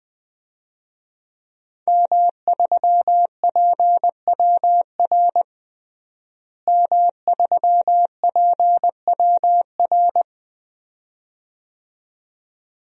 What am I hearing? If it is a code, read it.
Morse code, heard as a keyed tone, about 20 wpm, 700 Hz. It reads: M3PWR M3PWR
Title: DXCC-100 Call Signs - 5 Characters - Repeated 2x - 20wpm